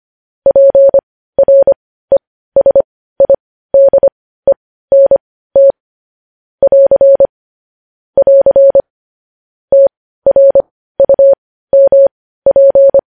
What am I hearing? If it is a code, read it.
Morse code, heard as a keyed tone, about 25 wpm, spaced as for 15 wpm, 550 Hz. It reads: PRESIDENT <AR> <AR> TRUMP
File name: podcast / News headlines-1x-25x15